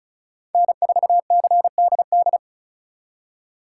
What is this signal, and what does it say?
Morse code, heard as a keyed tone, about 44 wpm, 700 Hz. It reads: N4CDD
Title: Speed Racing - USA 1x3 Call Signs 35wpm